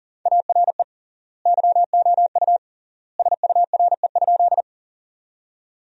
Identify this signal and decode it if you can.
Morse code, heard as a keyed tone, about 40 wpm, 700 Hz. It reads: ARE YOU SURE?
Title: 3x Word-Spacing - Sentences from Top 300 Words - 40wpm